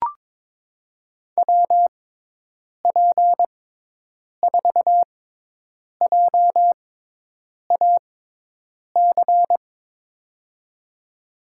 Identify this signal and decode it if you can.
Morse code, heard as a keyed tone, about 22 wpm, spaced as for 8 wpm, 700 Hz. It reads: WP4JAC
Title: USA Call Signs - 6 Characters - 6x Character Spacing - 22wpm